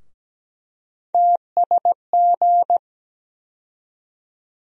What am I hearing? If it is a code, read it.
Morse code, heard as a keyed tone, about 17 wpm, 700 Hz. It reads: TSG
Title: USA Call Sign Postfixes - 3 Letters 17wpm